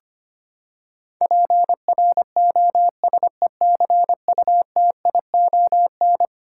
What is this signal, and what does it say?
Morse code, heard as a keyed tone, about 25 wpm, 700 Hz. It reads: PROSECUTION
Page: morse-code-ninja-practice-25wpm